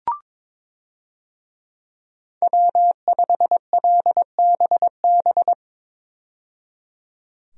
Morse code, heard as a keyed tone, about 23 wpm, 700 Hz. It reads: W5LBB
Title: USA Call Signs - 5 Characters - 22wpm